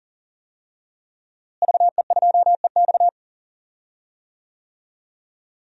Morse code, heard as a keyed tone, about 40 wpm, 700 Hz. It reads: VE2EX
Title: DXCC-100 Call Signs - 5 Characters - Repeated 2x - 40wpm